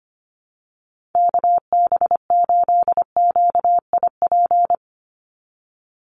Morse code, heard as a keyed tone, about 25 wpm, 700 Hz. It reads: KB8QIP
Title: USA Call Signs 25wpm